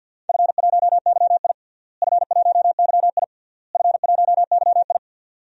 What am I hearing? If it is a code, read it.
Morse code, heard as a keyed tone, about 49 wpm, 700 Hz. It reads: F1YI F1YI F1YI
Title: DXCC-100 Call Signs - 4 Characters - Repeated 3x - 50wpm